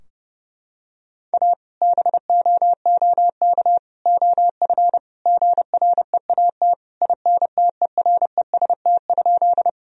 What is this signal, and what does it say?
Morse code, heard as a keyed tone, about 30 wpm, 700 Hz. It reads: A BOOK OF GREAT INTEREST?
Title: Sentences from Top 1000 Words - Review Part 2 - 30wpm